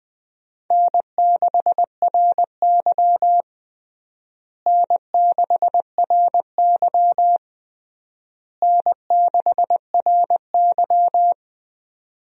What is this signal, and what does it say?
Morse code, heard as a keyed tone, about 20 wpm, 700 Hz. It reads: N6RY N6RY N6RY
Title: USA Call Signs - 4 Characters - Repeated 3x - 20wpm